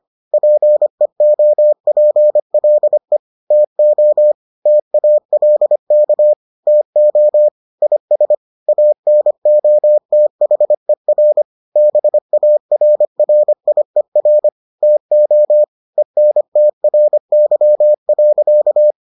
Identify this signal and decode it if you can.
Morse code, heard as a keyed tone, about 25 wpm, 600 Hz. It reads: PEOPLE TO TALK TO IS ANOTHER BARRIER TO ENTRY.